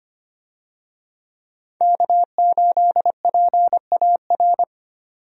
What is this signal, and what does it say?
Morse code, heard as a keyed tone, about 25 wpm, 700 Hz. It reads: K8PAR
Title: USA Call Signs - 5 Characters - 25wpm